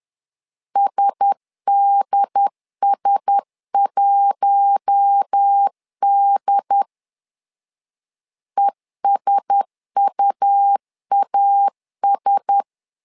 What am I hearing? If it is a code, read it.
Morse code, heard as a keyed tone, about 10 wpm, 800 Hz. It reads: SDS1D ESUAS
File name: LEZIONI / 4 Q1S 10wpm 2133